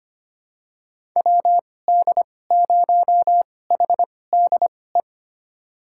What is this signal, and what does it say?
Morse code, heard as a keyed tone, about 25 wpm, spaced as for 18 wpm, 700 Hz. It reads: WD0HDE
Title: USA Call Signs - 6 Characters - 2x Character Spacing - 25wpm